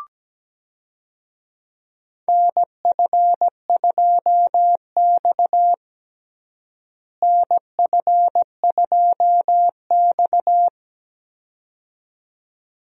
Morse code, heard as a keyed tone, about 17 wpm, 700 Hz. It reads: NF2X NF2X
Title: USA Call Signs - 4 Characters - Repeated 2x - 17wpm